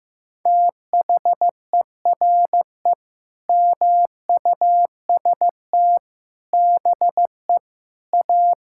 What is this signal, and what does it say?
Morse code, heard as a keyed tone, about 15 wpm, 700 Hz. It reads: THERE MUST BE A
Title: Sentences from Top 1000 Words - Review Part 6 - 15wpm